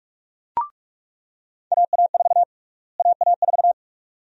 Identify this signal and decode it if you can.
Morse code, heard as a keyed tone, about 46 wpm, 700 Hz. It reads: AA4 AA4
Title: USA Call Sign Prefixes - Repeated 2x - 45wpm